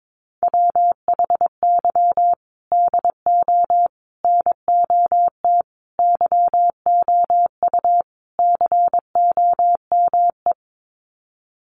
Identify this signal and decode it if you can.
Morse code, heard as a keyed tone, about 22 wpm, 700 Hz. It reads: WHY DO NOT YOU COME
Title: Sets of 5 Words Encore2 22wpm